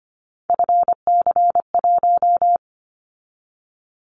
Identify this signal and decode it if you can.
Morse code, heard as a keyed tone, about 25 wpm, 700 Hz. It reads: FC1
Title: Callsign Prefixes for Top 50 DXCC - 25wpm